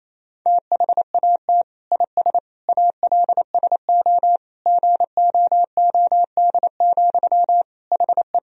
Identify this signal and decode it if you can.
Morse code, heard as a keyed tone, about 27 wpm, 700 Hz. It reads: THAT IS ALSO GOOD, HE
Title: Sentences from Top 1000 Words - Review Part 1 - 28wpm